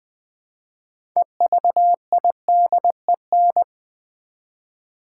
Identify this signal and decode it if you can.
Morse code, heard as a keyed tone, about 20 wpm, 700 Hz. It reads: EVIDEN
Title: Build A Word - 2 Letters At A Time - 8-Letter Words - 20wpm